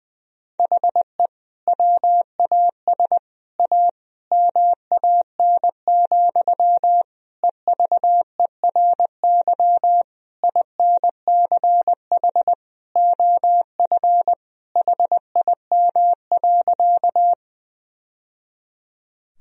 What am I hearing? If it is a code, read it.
Morse code, heard as a keyed tone, about 20 wpm, 700 Hz. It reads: HE WAS A MAN, EVERY INCH OF HIM.
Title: Sentences from Top 1000 Words - Review Part 3 - 20wpm